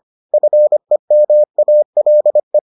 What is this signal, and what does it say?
Morse code, heard as a keyed tone, about 25 wpm, 600 Hz. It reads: FEMALE